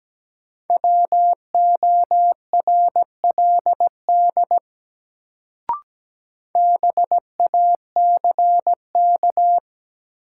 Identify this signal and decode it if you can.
Morse code, heard as a keyed tone, about 17 wpm, 700 Hz. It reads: WORLD BACK
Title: Sets of 2 Words - New Word Drill - 17wpm